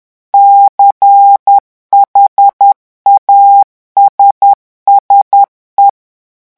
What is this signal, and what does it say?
Morse code, heard as a keyed tone, about 10 wpm, 800 Hz. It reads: CHASSE
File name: morse mp3 / revision francais 10 mots minute